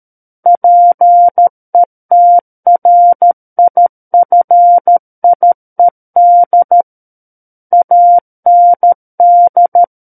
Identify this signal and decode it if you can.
Morse code, heard as a keyed tone, about 13 wpm, 700 Hz. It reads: PETRIFIED AND
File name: Book One / 04-Chapter 4 - 13 WPM